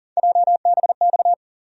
Morse code, heard as a keyed tone, about 40 wpm, 700 Hz. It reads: JBX
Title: USA Call Sign Postfixes - 3 Letters 40wpm